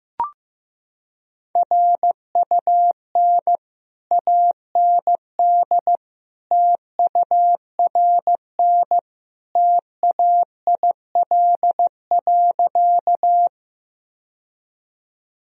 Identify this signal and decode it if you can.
Morse code, heard as a keyed tone, about 16 wpm, 700 Hz. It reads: RUN AND TURN TAIL.
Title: Sentences from Top 1000 Words - Review Part 3 - 15wpm